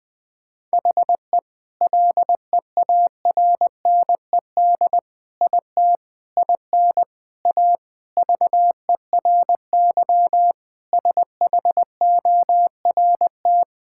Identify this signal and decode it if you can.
Morse code, heard as a keyed tone, about 21 wpm, 700 Hz. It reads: HE LEARNED IT IN A VERY SHORT
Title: Sentences with 7-Word Phrases - Spoken Beginning - 20wpm